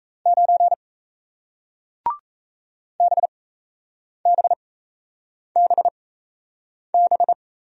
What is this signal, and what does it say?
Morse code, heard as a keyed tone, about 36 wpm, spaced as for 8 wpm, 700 Hz. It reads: 9 BBBB